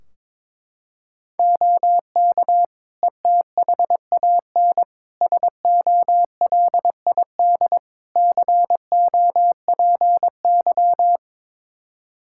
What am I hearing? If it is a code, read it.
Morse code, heard as a keyed tone, about 22 wpm, 700 Hz. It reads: OK ETHAN SOLID COPY
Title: Practice QSO - Exchange 3 - More Info - 22wpm